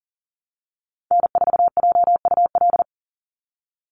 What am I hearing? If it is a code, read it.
Morse code, heard as a keyed tone, about 40 wpm, 700 Hz. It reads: N4JUL
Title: USA Call Signs 40wpm